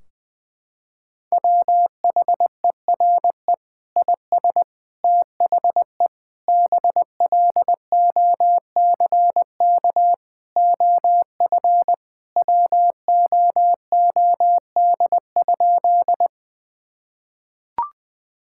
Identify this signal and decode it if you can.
Morse code, heard as a keyed tone, about 21 wpm, 700 Hz. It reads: WHERE IS THE BLOCK OF WOOD?
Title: Sentences from Top 1600 Words 20wpm